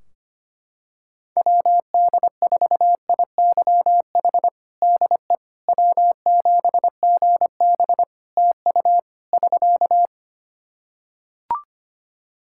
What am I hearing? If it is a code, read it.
Morse code, heard as a keyed tone, about 25 wpm, 700 Hz. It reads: WD4IYH DE W7GB TU <SK>
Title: Practice QSO - Exchange 4 - Ending - 25wpm